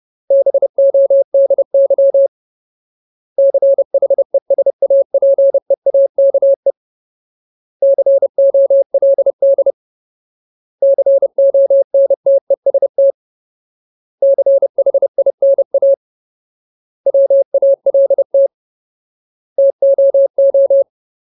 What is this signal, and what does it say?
Morse code, heard as a keyed tone, about 30 wpm, 550 Hz. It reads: BODY CHESAPEAKE COLD CONTEST CHINA WALT TOO
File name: Ultimate Word List / 30WPM Ultimate